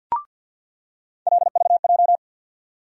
Morse code, heard as a keyed tone, about 53 wpm, 700 Hz. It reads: LVJ
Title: USA Call Sign Postfixes - 3 Letters 50wpm